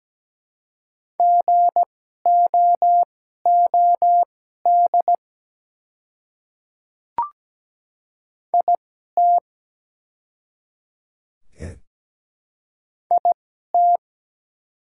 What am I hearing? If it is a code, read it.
Morse code, heard as a keyed tone, about 17 wpm, spaced as for 12 wpm, 700 Hz. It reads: GOOD IT IT
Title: 2x Character Spacing - Top 200 Words - 17wpm